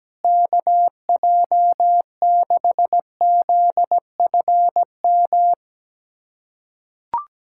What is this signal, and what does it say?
Morse code, heard as a keyed tone, about 17 wpm, 700 Hz. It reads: KJ6ZFM
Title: USA Callsign DE Callsign 17wpm